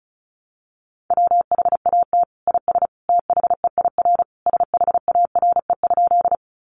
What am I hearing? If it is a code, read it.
Morse code, heard as a keyed tone, about 34 wpm, 700 Hz. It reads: WHAT IS THEIR SHARE?
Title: Sentences from Top 900 Words 35wpm